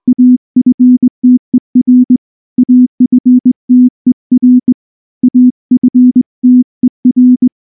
Morse code, heard as a keyed tone, about 20 wpm, 250 Hz. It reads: AFTER AFTER AFTER
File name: morse / morse-training-20WPM-7